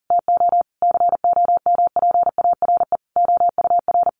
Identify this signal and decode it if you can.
Morse code, heard as a keyed tone, about 40 wpm, 700 Hz. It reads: TO COMPARE OUR
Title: Sentences from Top 1300 Words 40wpm